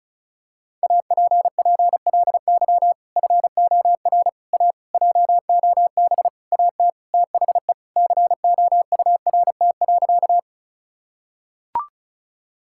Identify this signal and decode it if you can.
Morse code, heard as a keyed tone, about 36 wpm, 700 Hz. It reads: APPLY FOR A JOB AT THE COURT.
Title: Sentences from Top 1600 Words 35wpm